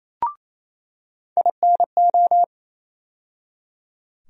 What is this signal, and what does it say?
Morse code, heard as a keyed tone, about 28 wpm, 700 Hz. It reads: INO
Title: USA Call Sign Postfixes - 3 Letters 28wpm